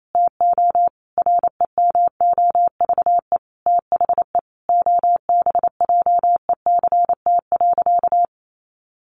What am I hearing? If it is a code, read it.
Morse code, heard as a keyed tone, about 28 wpm, 700 Hz. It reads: TO REMOVE THE OBJECT.
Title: Sentences from Top 800 Words 28wpm